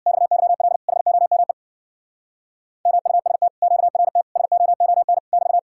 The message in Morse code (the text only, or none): XYL HOME MUST QRT SOON <BT>